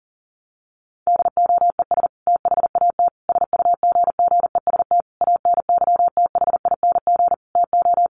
Morse code, heard as a keyed tone, about 40 wpm, 700 Hz. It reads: DOES THAT SUGGEST ANYTHING TO
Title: Sentences from Top 1300 Words 40wpm